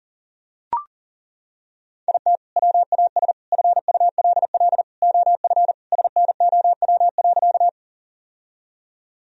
Morse code, heard as a keyed tone, about 40 wpm, 700 Hz. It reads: IT WAS FULL OF SNOW.
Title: Sentences from Top 1000 Words - Review Part 3 - 40wpm